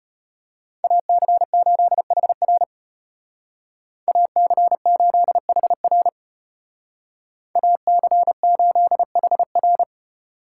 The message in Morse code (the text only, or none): AC8HR AC8HR AC8HR